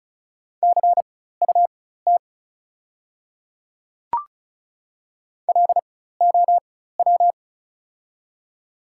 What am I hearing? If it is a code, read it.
Morse code, heard as a keyed tone, about 36 wpm, spaced as for 16 wpm, 690 Hz. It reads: CUT LOW
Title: 4x Character Spacing - Top 300 Words - 35wpm